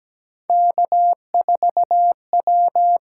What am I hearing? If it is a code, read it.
Morse code, heard as a keyed tone, about 17 wpm, 700 Hz. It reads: K4W
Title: USA Call Signs - 8x Character Spacing - 17wpm